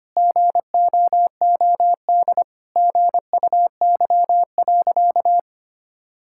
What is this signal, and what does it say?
Morse code, heard as a keyed tone, about 25 wpm, 700 Hz. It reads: GOOD GUY.
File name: practice / 25wpm / Sentences with 3-Word Phrases - 25wpm